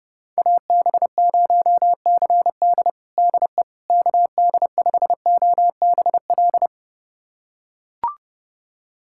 Morse code, heard as a keyed tone, about 32 wpm, 700 Hz. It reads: AB0CD DE KD5OBL
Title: USA Callsign DE Callsign 30wpm